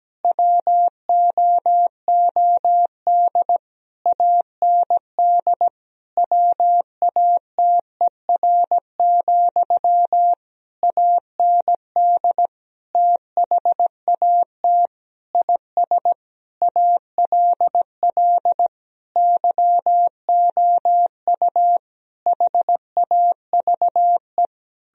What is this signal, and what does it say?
Morse code, heard as a keyed tone, about 17 wpm, 700 Hz. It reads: WOOD AND WATER, AND THAT IS ALL YOU HAVE
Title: Sentences from Top 1000 Words - Review Part 2 - 17wpm